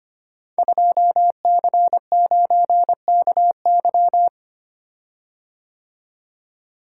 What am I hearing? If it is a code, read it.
Morse code, heard as a keyed tone, about 25 wpm, 700 Hz. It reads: 2C9KY
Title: Code Groups - 5 Letter:Numbers - 25wpm